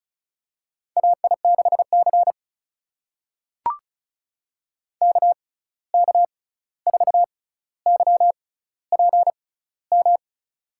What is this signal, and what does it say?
Morse code, heard as a keyed tone, about 36 wpm, spaced as for 11 wpm, 700 Hz. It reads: AI6CK K4YPM
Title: USA Call Signs - 6x Character Spacing - 35wpm